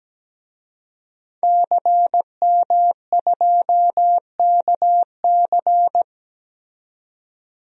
Morse code, heard as a keyed tone, about 17 wpm, 700 Hz. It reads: CM2KC
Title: DXCC-100 Call Signs - 5 Characters - 17wpm